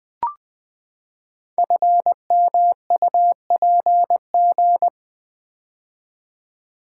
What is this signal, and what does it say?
Morse code, heard as a keyed tone, about 20 wpm, 700 Hz. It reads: FMUPG
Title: Code Groups - 5 Letters - 20wpm